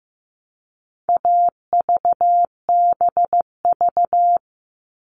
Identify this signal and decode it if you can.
Morse code, heard as a keyed tone, about 15 wpm, 700 Hz. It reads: AVBV
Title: All Cut Numbers - 4-Digit 15wpm